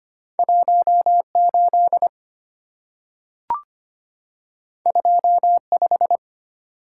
Morse code, heard as a keyed tone, about 25 wpm, 700 Hz. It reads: 18 25
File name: Two Letter-Number Combinations - 25wpm